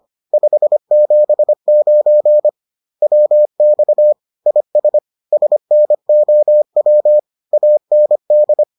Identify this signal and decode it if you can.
Morse code, heard as a keyed tone, about 25 wpm, 600 Hz. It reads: 579 WX IS SNOW AND